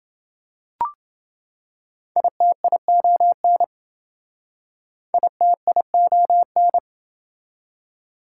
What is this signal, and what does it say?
Morse code, heard as a keyed tone, about 28 wpm, 700 Hz. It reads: ITION ITION